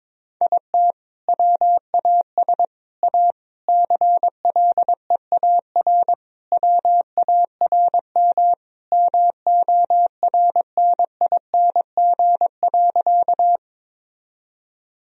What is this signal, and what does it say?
Morse code, heard as a keyed tone, about 21 wpm, 700 Hz. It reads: IT WAS A CLEAR WARM MORNING.
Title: Sentences from Top 1000 Words - Review Part 2 - 22wpm